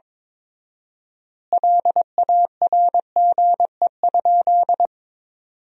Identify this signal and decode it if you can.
Morse code, heard as a keyed tone, about 22 wpm, 700 Hz. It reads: LARGE?
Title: 4x Word-Spacing - Sentences from Top 200 Words - 22wpm